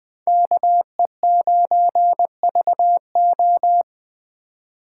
Morse code, heard as a keyed tone, about 21 wpm, 700 Hz. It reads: KE9VO